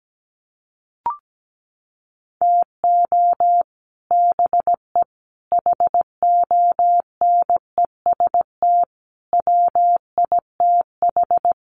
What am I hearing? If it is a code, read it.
Morse code, heard as a keyed tone, about 17 wpm, 700 Hz. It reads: TO BE HONEST WITH
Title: Sets of 5 Words Encore 17wpm